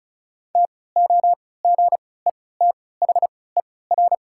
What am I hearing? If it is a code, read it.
Morse code, heard as a keyed tone, about 34 wpm, spaced as for 20 wpm, 690 Hz. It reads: TOGETHER
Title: 3x Character Spacing - Top 500 Words - 35wpm